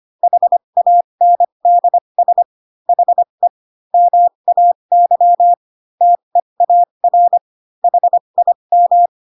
Morse code, heard as a keyed tone, about 25 wpm, spaced as for 22 wpm, 700 Hz. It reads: HANDS HE MAY TEAR HIM
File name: Pinocchio / Pinocchio ch3 18 25 ik1hge